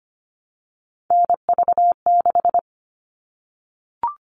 Morse code, heard as a keyed tone, about 25 wpm, 700 Hz. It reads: NV6
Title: All Cut Numbers - 3-Digit 25wpm